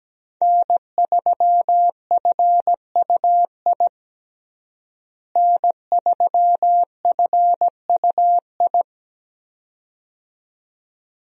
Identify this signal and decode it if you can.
Morse code, heard as a keyed tone, about 17 wpm, 700 Hz. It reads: N3FUI N3FUI
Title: USA Call Signs - 5 Characters - Repeated 2x - 17wpm